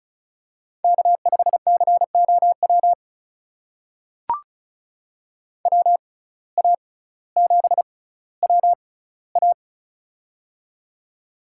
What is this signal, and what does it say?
Morse code, heard as a keyed tone, about 36 wpm, spaced as for 18 wpm, 700 Hz. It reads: K5COW WA7WA
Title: USA Call Signs - 5 Characters - 6x Character Spacing - 35wpm